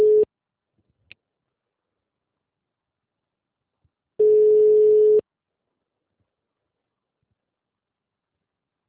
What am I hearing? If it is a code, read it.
Ringback tone (Europe, 425 Hz).